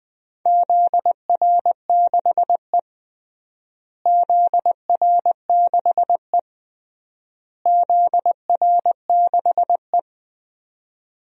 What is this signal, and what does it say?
Morse code, heard as a keyed tone, about 21 wpm, 700 Hz. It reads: ZR6E ZR6E ZR6E
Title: DXCC-100 Call Signs - 4 Characters - Repeated 3x - 20wpm